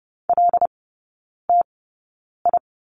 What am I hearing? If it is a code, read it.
Morse code, heard as a keyed tone, about 30 wpm, spaced as for 9 wpm, 700 Hz. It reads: LTI
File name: 081 Lesson 14e - Number 2 - Three Characters 30wpm